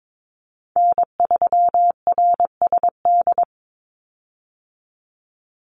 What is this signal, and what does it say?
Morse code, heard as a keyed tone, about 22 wpm, 700 Hz. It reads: N3RSD
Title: Call Signs Repeated 3x - 22wpm